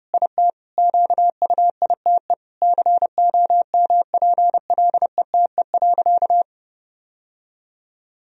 Morse code, heard as a keyed tone, about 32 wpm, 700 Hz. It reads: IT QUITE COMPLETE.
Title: Sentences from Top 1000 Words - Review Part 2 - 30wpm